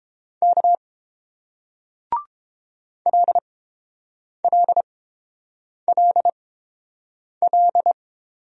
Morse code, heard as a keyed tone, about 30 wpm, spaced as for 7 wpm, 700 Hz. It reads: K LLLL